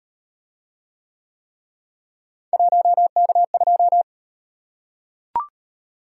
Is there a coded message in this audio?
Yes: Morse code from 2.5 to 4.0 s, heard as a keyed tone, about 38 wpm, 700 Hz. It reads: 1K2